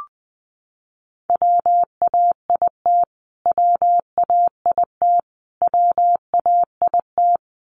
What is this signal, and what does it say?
Morse code, heard as a keyed tone, about 20 wpm, 700 Hz. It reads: WAIT WAIT WAIT
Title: Top 500 Words Repeated 20wpm